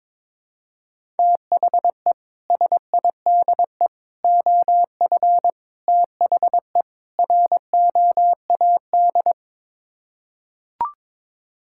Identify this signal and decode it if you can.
Morse code, heard as a keyed tone, about 22 wpm, 700 Hz. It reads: THE SIDE OF THE ROAD
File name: practice / 22wpm / Build A Phrase - 5-Word Phrases - 22wpm